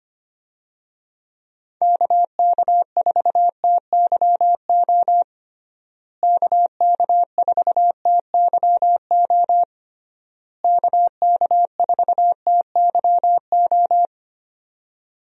Morse code, heard as a keyed tone, about 25 wpm, 700 Hz. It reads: KK4TYO KK4TYO KK4TYO
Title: USA Call Signs - 6 Characters - Repeated 3x - 25wpm